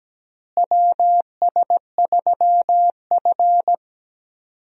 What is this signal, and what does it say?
Morse code, heard as a keyed tone, about 17 wpm, 700 Hz. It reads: WS3F
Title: USA Call Signs - 4 Characters - Repeated 2x - 17wpm